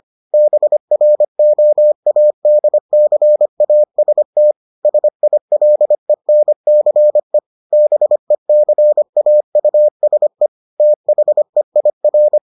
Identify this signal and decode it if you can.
Morse code, heard as a keyed tone, about 25 wpm, 600 Hz. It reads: BROADCAST SILENCE BECAUSE THEIR